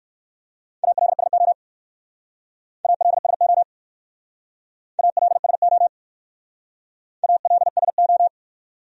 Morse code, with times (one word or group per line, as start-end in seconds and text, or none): 0.8-1.5 ALSO
2.9-3.6 ALSO
5.0-5.9 ALSO
7.2-8.3 ALSO